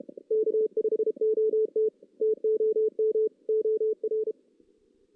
Morse code, heard as a keyed tone, about 32 wpm, 440 Hz. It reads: K5OT TOM OR
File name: 2016 JAN CW NAQP / 2016 Jan naqp k5ot - 0-3